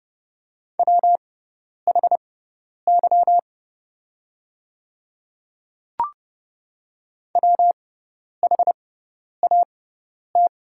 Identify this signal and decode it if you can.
Morse code, heard as a keyed tone, about 31 wpm, spaced as for 10 wpm, 700 Hz. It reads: WHY WHAT
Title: 6x Character Spacing - Top 200 Words - 30wpm